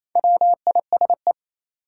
Morse code, heard as a keyed tone, about 28 wpm, 700 Hz. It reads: WISE
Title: Sentences from Top 1600 Words - New Word Drill - 28wpm